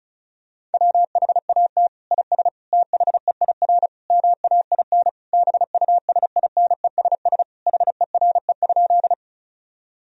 Morse code, heard as a keyed tone, about 35 wpm, 700 Hz. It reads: WHAT IS THEIR MAIN BUSINESS HERE?
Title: Sentences from Top 1000 Words - Review Part 2 - 35wpm